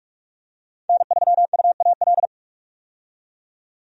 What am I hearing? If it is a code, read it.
Morse code, heard as a keyed tone, about 45 wpm, 700 Hz. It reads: N3UAL
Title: USA Call Signs - 5 Characters - 45wpm